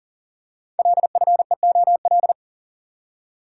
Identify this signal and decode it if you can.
Morse code, heard as a keyed tone, about 40 wpm, 700 Hz. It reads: LFEOL
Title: Code Groups - 5 Letters - 40wpm